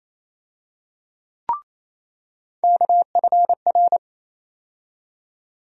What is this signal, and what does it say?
Morse code, heard as a keyed tone, about 28 wpm, 700 Hz. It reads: KFR
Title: USA Call Sign Postfixes - 3 Letters 28wpm